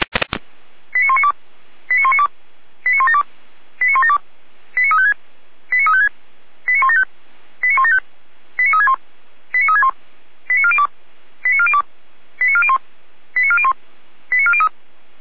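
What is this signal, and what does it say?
Five-tone selective call (ZVEI), heard as quick runs of five tones, tones of about 70 ms. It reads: 89192 89192 89172 89172 89356 89356 89175 89175 89361 89361 89402 89402 89401 89401 89403
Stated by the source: ZVEI five-tone selective call received by SDR, from feuerwehr-orlamuende.org fire brigade server.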